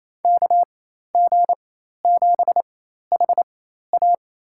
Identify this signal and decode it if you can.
Morse code, heard as a keyed tone, about 27 wpm, spaced as for 13 wpm, 700 Hz. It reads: KG7HA
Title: USA Call Signs - 5 Characters - 4x Character Spacing - 28wpm